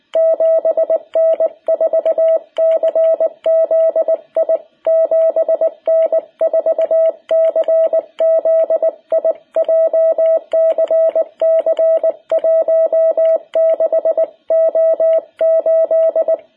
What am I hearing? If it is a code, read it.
Morse code, heard as a keyed tone, about 19 wpm, 660 Hz. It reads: 7N4CZI 7N4CZI JCC 16O8
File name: jh8cbh / 2021cw